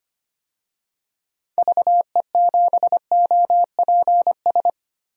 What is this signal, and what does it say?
Morse code, heard as a keyed tone, about 25 wpm, 700 Hz. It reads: VE7OPS
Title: DXCC-100 Call Signs - 6 Characters - 25wpm